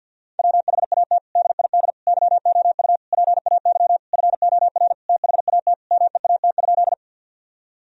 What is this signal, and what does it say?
Morse code, heard as a keyed tone, about 52 wpm, 700 Hz. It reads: WHAT DID YOU PAY FOR THAT MEAT?
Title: Sentences from Top 1000 Words - Review Part 3 - 50wpm